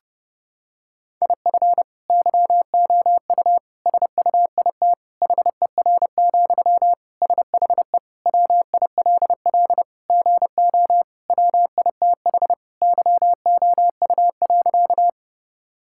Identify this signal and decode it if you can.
Morse code, heard as a keyed tone, about 30 wpm, 700 Hz. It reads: IF YOU SUIT HER, SHE WILL GO WITH YOU.
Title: Sentences from Top 1600 Words 30wpm